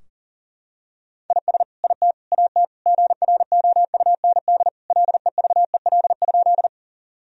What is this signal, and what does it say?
Morse code, heard as a keyed tone, about 37 wpm, 700 Hz. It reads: IS IT AT GROUND LEVEL?
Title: Sentences from Top 1000 Words - Review Part 4 - 40wpm